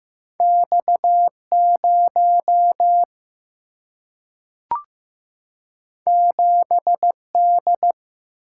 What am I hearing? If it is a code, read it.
Morse code, heard as a keyed tone, about 15 wpm, 700 Hz. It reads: X 0 7 D
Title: Single-Letter Single-Number 15wpm